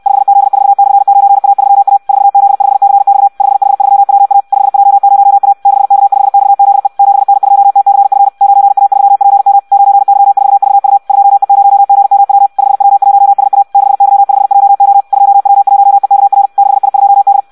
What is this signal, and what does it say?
Morse code, heard as a keyed tone, about 64 wpm, 800 Hz. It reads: 5Z4/9A3A 4Z5CX 5H3WA 5J0A 6K5XZE 7N2TCF 9A3PM 9Y4VU 2E0GMW 5W1SA 6Y5JO 2W1EKR 6I2D
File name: Audio practica CW / qrq-tx-66 2